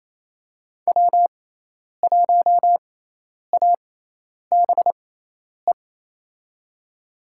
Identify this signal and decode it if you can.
Morse code, heard as a keyed tone, about 28 wpm, spaced as for 10 wpm, 700 Hz. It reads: W1ABE
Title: USA Call Signs - 5 Characters - 6x Character Spacing - 28wpm